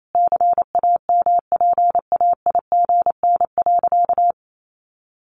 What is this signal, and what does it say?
Morse code, heard as a keyed tone, about 28 wpm, 700 Hz. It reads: CAMPAIGN.
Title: Sentences from Top 1200 Words 28wpm